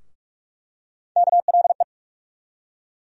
Morse code, heard as a keyed tone, about 46 wpm, 700 Hz. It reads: KLE
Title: USA Call Sign Postfixes - 3 Letters 45wpm